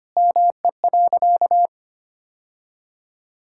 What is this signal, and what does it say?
Morse code, heard as a keyed tone, about 25 wpm, 700 Hz. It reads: ME.
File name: practice / 25wpm / Sentences with 3-Word Phrases - 25wpm